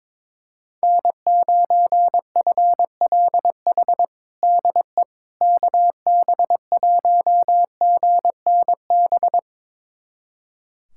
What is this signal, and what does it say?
Morse code, heard as a keyed tone, about 22 wpm, 700 Hz. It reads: N9FLH DE KB1GNB
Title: USA Callsign DE Callsign 22wpm